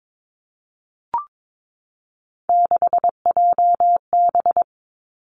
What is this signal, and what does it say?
Morse code, heard as a keyed tone, about 22 wpm, 700 Hz. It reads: 6JB
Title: Three Challenging Letter-Number Combinations 22wpm